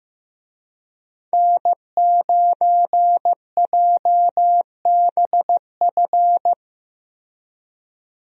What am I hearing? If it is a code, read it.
Morse code, heard as a keyed tone, about 15 wpm, 700 Hz. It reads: N9JBF
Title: USA Call Signs - 5 Characters - Repeated 2x - 15wpm